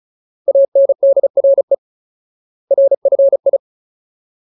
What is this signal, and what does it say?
Morse code, heard as a keyed tone, about 34 wpm, 550 Hz. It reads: ANDRE RFI
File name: Ultimate Word List / 35WPM Ultimate